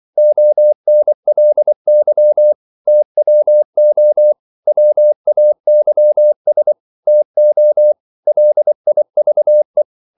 Morse code, heard as a keyed tone, about 24 wpm, 600 Hz. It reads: ONLY TWO WAYS TO LIVE
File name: CW / Practice / EnglishQuotations0424